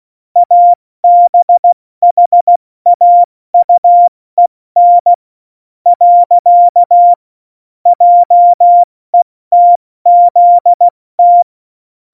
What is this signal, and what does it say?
Morse code, heard as a keyed tone, about 16 wpm, 700 Hz. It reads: ABHAUEN . JETZT
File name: Morsekurs / Witze / Witz 25. September 2018, 70 F80